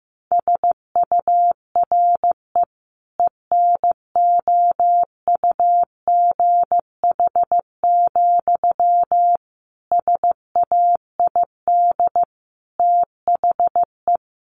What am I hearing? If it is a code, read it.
Morse code, heard as a keyed tone, about 15 wpm, 700 Hz. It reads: SURE ENOUGH, SAID THE
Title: Sentences from Top 600 Words 15wpm